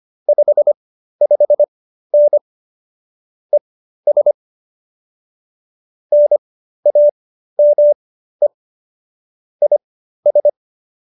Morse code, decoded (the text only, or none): 55N ES NAME IS